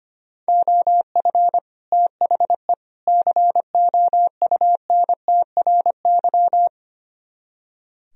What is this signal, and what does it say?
Morse code, heard as a keyed tone, about 24 wpm, 700 Hz. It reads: OF THE COUNTRY
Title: Speed-Racing - Sets of 3 Words 25wpm